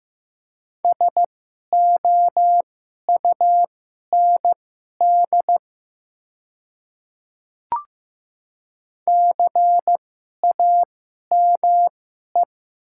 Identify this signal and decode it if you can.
Morse code, heard as a keyed tone, about 15 wpm, spaced as for 11 wpm, 700 Hz. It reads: SOUND CAME
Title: 2x Character Spacing - Top 300 Words - 15wpm